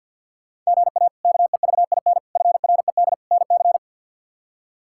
Morse code, heard as a keyed tone, about 50 wpm, 700 Hz. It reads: GA KEVIN FRED NC
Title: K1USN SST QSO Practice - 50wpm